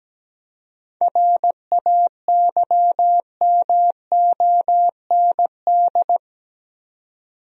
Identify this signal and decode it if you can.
Morse code, heard as a keyed tone, about 17 wpm, 700 Hz. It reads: RAYMOND
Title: USA Common Names - 17wpm